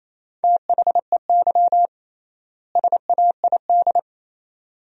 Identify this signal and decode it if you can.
Morse code, heard as a keyed tone, about 28 wpm, 700 Hz. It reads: THEY SAID
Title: 3x Word-Spacing - Sentences from Top 100 Words - 28wpm